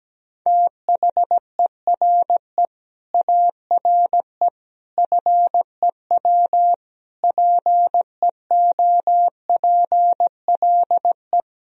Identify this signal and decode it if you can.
Morse code, heard as a keyed tone, about 17 wpm, 700 Hz. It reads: THERE ARE FEW PEOPLE
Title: Sentences from Top 1000 Words - Review Part 4 - 17wpm